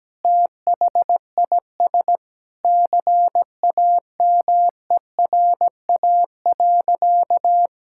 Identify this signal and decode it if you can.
Morse code, heard as a keyed tone, about 17 wpm, 700 Hz. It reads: THIS CAMERA.
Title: Sentences from Top 1000 Words - Review Part 5 - 17wpm